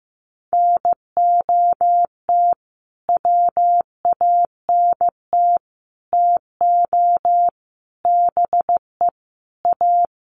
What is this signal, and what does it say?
Morse code, heard as a keyed tone, about 15 wpm, 700 Hz. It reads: NOT WANT TO BE A
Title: Sets of 5 Words Encore 15wpm